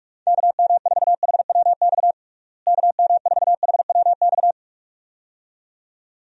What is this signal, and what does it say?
Morse code, heard as a keyed tone, about 42 wpm, 700 Hz. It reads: KM4HWX KM4HWX
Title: USA Call Signs - 6 Characters - Repeated 2x - 45wpm